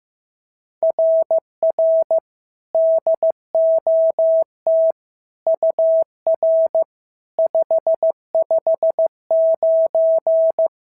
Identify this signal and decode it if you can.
Morse code, heard as a keyed tone, about 15 wpm, 650 Hz. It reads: RR DOT UR 559
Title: SOTA QSO Practice 15wpm